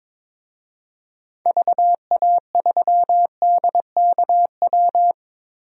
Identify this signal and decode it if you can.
Morse code, heard as a keyed tone, about 22 wpm, 700 Hz. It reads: VA3DKW
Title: DXCC-100 Call Signs - 6 Characters - Repeated 3x - 22wpm